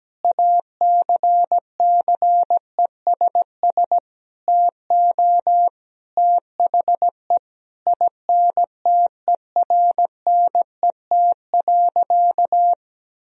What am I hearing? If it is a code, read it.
Morse code, heard as a keyed tone, about 17 wpm, 700 Hz. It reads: ACCESS TO THE INTERNET.
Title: Sentences from Top 1000 Words - Review Part 4 - 17wpm